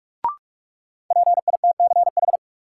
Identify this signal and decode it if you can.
Morse code, heard as a keyed tone, about 46 wpm, 700 Hz. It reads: PITCH
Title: Sentences from Top 1300 Words - New Word Drill - 45wpm